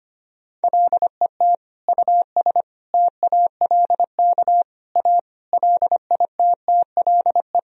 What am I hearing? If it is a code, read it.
Morse code, heard as a keyed tone, about 25 wpm, 700 Hz. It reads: LET US TALK A LITTLE
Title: Sentences from Top 400 Words and Abbreviations 25wpm